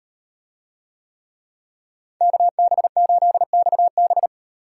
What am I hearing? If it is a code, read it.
Morse code, heard as a keyed tone, about 38 wpm, 700 Hz. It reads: KB8XB